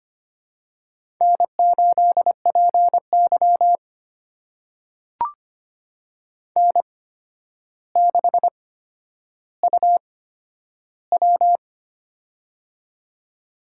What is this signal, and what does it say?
Morse code, heard as a keyed tone, about 25 wpm, spaced as for 11 wpm, 700 Hz. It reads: N8PY N6UW
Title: USA Call Signs - 4 Characters - 8x Character Spacing - 25wpm